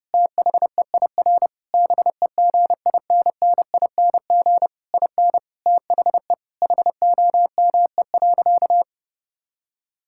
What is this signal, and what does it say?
Morse code, heard as a keyed tone, about 30 wpm, 700 Hz. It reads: THEIR BEGINNING IN THE HOME.
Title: Sentences from Top 1000 Words - Review Part 6 - 30wpm